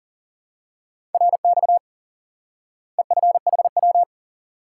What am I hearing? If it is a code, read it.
Morse code, heard as a keyed tone, about 40 wpm, 700 Hz. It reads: RX EFHW
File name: IQE Course 3 - Lesson 89 - YAGI - Two Elements - 40wpm